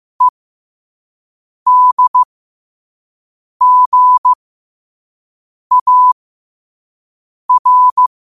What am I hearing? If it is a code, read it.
Morse code, heard as a keyed tone, about 15 wpm, spaced as for 5 wpm, 1000 Hz. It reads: EDGAR
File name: qotdfiles / 5wpm